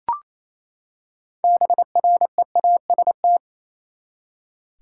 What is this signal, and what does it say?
Morse code, heard as a keyed tone, about 28 wpm, 700 Hz. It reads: BREAST
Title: Six Letter Words - 28wpm